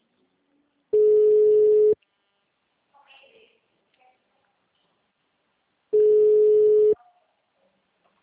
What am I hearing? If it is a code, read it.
Ringback tone (Europe, 425 Hz).